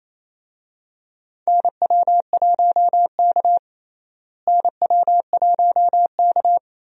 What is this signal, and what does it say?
Morse code, heard as a keyed tone, about 28 wpm, 700 Hz. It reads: NW1K NW1K
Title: USA Call Signs - 4 Characters - Repeated 3x - 28wpm